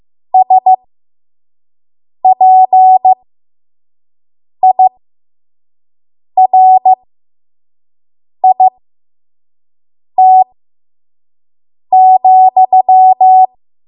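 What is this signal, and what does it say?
Morse code, heard as a keyed tone, about 15 wpm, spaced as for 5 wpm, 750 Hz. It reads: SPIRIT,
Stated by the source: 5 WPM Morse code practice.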